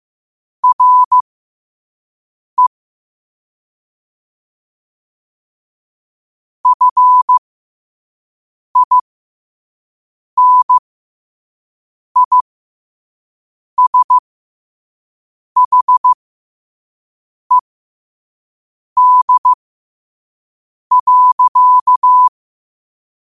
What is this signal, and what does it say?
Morse code, heard as a keyed tone, about 15 wpm, spaced as for 5 wpm, 1000 Hz. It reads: RE FINISHED.